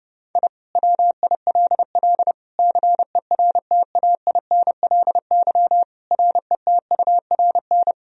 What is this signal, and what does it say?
Morse code, heard as a keyed tone, about 32 wpm, 700 Hz. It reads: I WILL CERTAINLY RETURN
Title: Sentences from Top 1000 Words - Review Part 4 - 30wpm